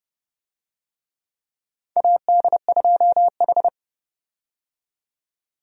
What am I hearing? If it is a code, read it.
Morse code, heard as a keyed tone, about 30 wpm, 700 Hz. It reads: AD2H
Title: USA Call Signs - 4 Characters - 30wpm